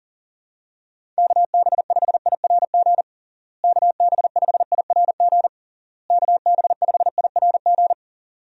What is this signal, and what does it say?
Morse code, heard as a keyed tone, about 40 wpm, 700 Hz. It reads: KB5IRG KB5IRG KB5IRG
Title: USA Call Signs - 6 Characters - Repeated 3x - 40wpm